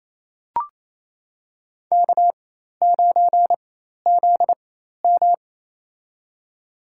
Morse code, heard as a keyed tone, about 28 wpm, spaced as for 13 wpm, 700 Hz. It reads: K9ZM
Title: USA Call Signs - 4x Character Spacing - 28wpm